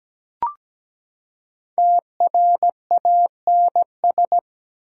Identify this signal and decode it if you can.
Morse code, heard as a keyed tone, about 17 wpm, 700 Hz. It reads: TRANS
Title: Prefixes and Suffixes 17wpm